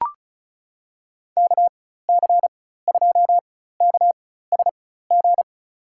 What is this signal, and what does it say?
Morse code, heard as a keyed tone, about 34 wpm, spaced as for 16 wpm, 700 Hz. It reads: KC2KSG
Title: USA Call Signs - 6 Characters - 4x Character Spacing - 35wpm